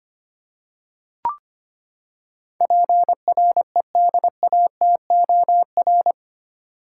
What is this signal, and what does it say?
Morse code, heard as a keyed tone, about 25 wpm, 700 Hz. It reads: PREDATOR
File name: Eight Letter Words - 25wpm